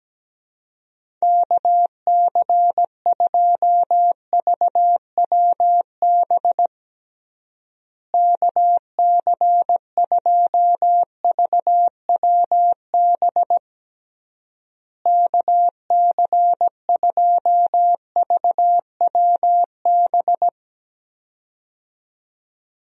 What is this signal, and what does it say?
Morse code, heard as a keyed tone, about 17 wpm, 700 Hz. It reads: KC2VWB KC2VWB KC2VWB
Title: USA Call Signs - 6 Characters - Repeated 3x - 17wpm